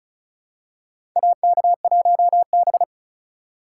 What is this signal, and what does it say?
Morse code, heard as a keyed tone, about 36 wpm, 700 Hz. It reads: AK1B
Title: USA Call Signs - 2x Character Spacing - 35wpm